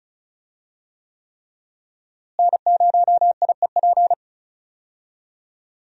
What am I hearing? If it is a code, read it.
Morse code, heard as a keyed tone, about 36 wpm, 700 Hz. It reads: N0IEP
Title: USA Call Signs - 5 Characters - Repeated 3x - 35wpm